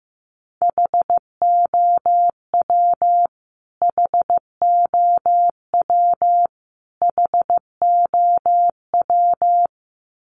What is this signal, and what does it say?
Morse code, heard as a keyed tone, about 15 wpm, 700 Hz. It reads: HOW HOW HOW
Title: Top 200 Words Repeated 15wpm